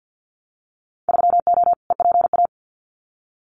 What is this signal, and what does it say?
Morse code, heard as a keyed tone, about 50 wpm, 700 Hz. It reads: 3O EPA
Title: Winter Field Day Exchanges Encore 50wpm